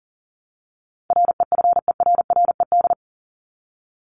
Morse code, heard as a keyed tone, about 40 wpm, 700 Hz. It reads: REFERRED
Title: ED Suffix - Top 100 Words with Base Words 40wpm